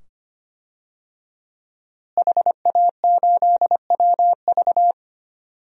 Morse code, heard as a keyed tone, about 25 wpm, 700 Hz. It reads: HA8WV
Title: DXCC-100 Call Signs - 5 Characters - Repeated 2x - 25wpm